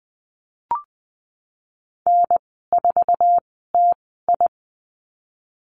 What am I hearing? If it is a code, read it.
Morse code, heard as a keyed tone, about 20 wpm, spaced as for 15 wpm, 700 Hz. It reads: N4TI
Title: 048 Lesson 9g - Number 4 - Call Signs 20wpm